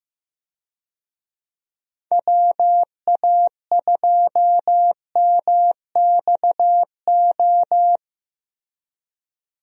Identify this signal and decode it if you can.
Morse code, heard as a keyed tone, about 15 wpm, 700 Hz. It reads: WA2MXO
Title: USA Call Signs - 6 Characters - Repeated 2x - 15wpm